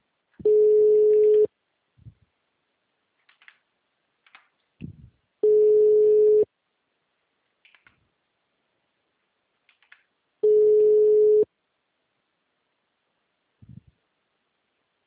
Ringback tone (Europe, 425 Hz).